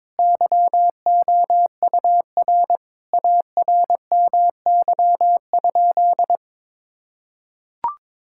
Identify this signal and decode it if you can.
Morse code, heard as a keyed tone, about 22 wpm, 700 Hz. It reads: YOUR ARMY?
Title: Sentences from Top 1000 Words - Review Part 5 - 22wpm